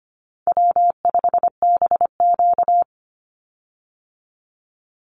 Morse code, heard as a keyed tone, about 25 wpm, 700 Hz. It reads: W5BQ
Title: Easy Call Signs Repeated 2x - 25wpm